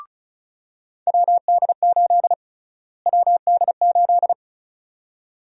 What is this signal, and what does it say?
Morse code, heard as a keyed tone, about 34 wpm, 700 Hz. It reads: WD8 WD8
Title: USA Call Sign Prefixes - Repeated 2x - 35wpm